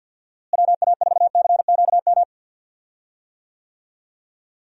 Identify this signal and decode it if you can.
Morse code, heard as a keyed tone, about 50 wpm, 700 Hz. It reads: WA4CQM
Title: USA Call Signs - 6 Characters - Repeated 2x - 50wpm